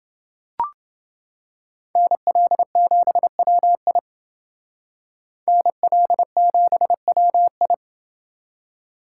Morse code, heard as a keyed tone, about 28 wpm, 700 Hz. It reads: NL7WI NL7WI